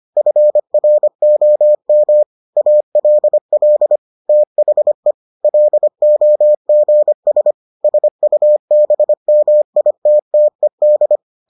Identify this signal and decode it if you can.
Morse code, heard as a keyed tone, about 25 wpm, 600 Hz. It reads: FROM ALL THE LOGS SUBMITTED